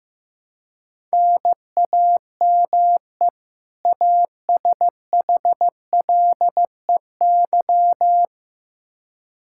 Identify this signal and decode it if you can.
Morse code, heard as a keyed tone, about 15 wpm, 700 Hz. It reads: NAME ASHLEY
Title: Practice QSO - Exchange 1 - Beginning - 15wpm